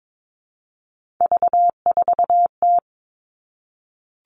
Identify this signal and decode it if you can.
Morse code, heard as a keyed tone, about 22 wpm, 700 Hz. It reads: V4T
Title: Three Challenging Letter-Number Combinations 22wpm